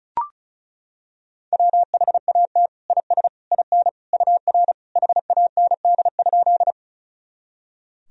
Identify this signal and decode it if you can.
Morse code, heard as a keyed tone, about 36 wpm, 700 Hz. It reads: WHAT IS IN UR HAND?
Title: Sentences from Top 200 Words and Abbreviations 35wpm